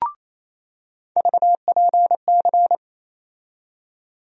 Morse code, heard as a keyed tone, about 28 wpm, 700 Hz. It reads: VPC
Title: USA Call Sign Postfixes - 3 Letters 28wpm